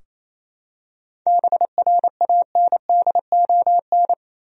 Morse code, heard as a keyed tone, about 28 wpm, 700 Hz. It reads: BRANDON
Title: USA Common Names - 28wpm